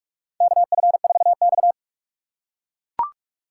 Morse code, heard as a keyed tone, about 44 wpm, 700 Hz. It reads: KF4X
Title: USA Call Signs - 4 Characters - 2x Character Spacing - 45wpm